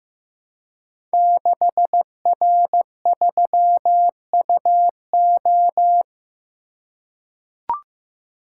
Morse code, heard as a keyed tone, about 15 wpm, 700 Hz. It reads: 6R3UO
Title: Code Groups - 5 Letter:Numbers - 15wpm